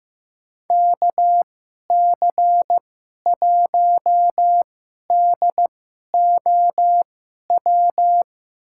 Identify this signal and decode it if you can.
Morse code, heard as a keyed tone, about 15 wpm, spaced as for 11 wpm, 700 Hz. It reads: KC1DOW
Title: USA Call Signs - 2x Character Spacing - 15wpm